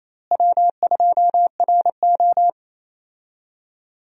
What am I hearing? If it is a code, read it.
Morse code, heard as a keyed tone, about 28 wpm, 700 Hz. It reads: W2RO
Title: USA Call Signs - 4 Characters - Repeated 3x - 28wpm